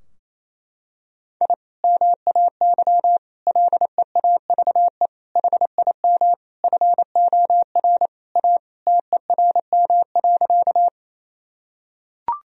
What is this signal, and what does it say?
Morse code, heard as a keyed tone, about 28 wpm, 700 Hz. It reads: I MAY LEAVE HIM FOR A TERM.
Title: Sentences from Top 1000 Words - Review Part 4 - 28wpm